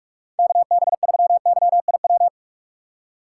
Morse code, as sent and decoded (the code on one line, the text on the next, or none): -.- -... ...-- -.-- .. .--
KB3YIW